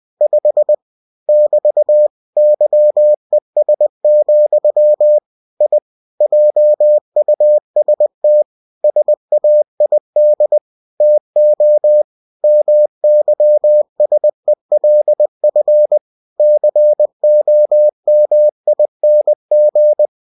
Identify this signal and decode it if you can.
Morse code, heard as a keyed tone, about 20 wpm, 600 Hz. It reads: 5 <BT> YES, I JUST SAID TO MYSELF COMING
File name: cw-books / winnie-the-pooh / winnie-20wpm-page-0005